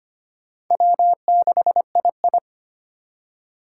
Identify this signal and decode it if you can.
Morse code, heard as a keyed tone, about 25 wpm, 700 Hz. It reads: W6II
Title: USA Call Signs - 4 Characters - 4x Character Spacing - 25wpm